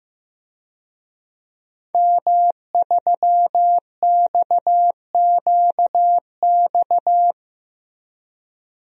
Morse code, heard as a keyed tone, about 16 wpm, 700 Hz. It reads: M3XQX
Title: DXCC-100 Call Signs - 5 Characters - Repeated 3x - 15wpm